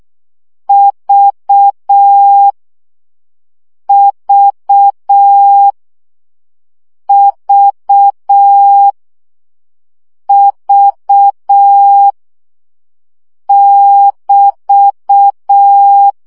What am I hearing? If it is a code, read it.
Morse code, heard as a keyed tone, about 6 wpm, 800 Hz. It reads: V V V V <BT>
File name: CW gyakorlas v 5 baud